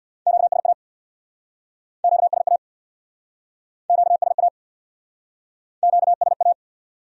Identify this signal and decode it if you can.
Morse code, heard as a keyed tone, about 64 wpm, 700 Hz. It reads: QSA QSA QSA QSA